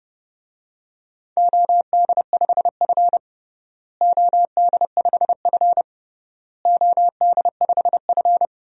Morse code, heard as a keyed tone, about 30 wpm, 700 Hz. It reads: OD5F OD5F OD5F
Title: DXCC-100 Call Signs - 4 Characters - Repeated 3x - 30wpm